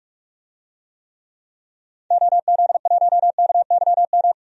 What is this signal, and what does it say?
Morse code, heard as a keyed tone, about 46 wpm, 700 Hz. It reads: OZ1KYM
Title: DXCC-100 Call Signs - 6 Characters - 45wpm